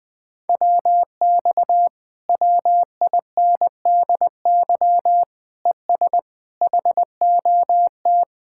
Morse code, heard as a keyed tone, about 20 wpm, 700 Hz. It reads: WX WINDY ES HOT
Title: Speed-Racing Common QSO Phrases Encore - 20wpm